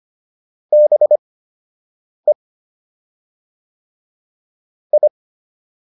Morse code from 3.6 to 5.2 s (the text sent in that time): I